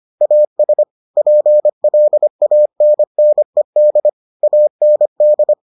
Morse code, heard as a keyed tone, about 25 wpm, 600 Hz. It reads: AS PLANNED AND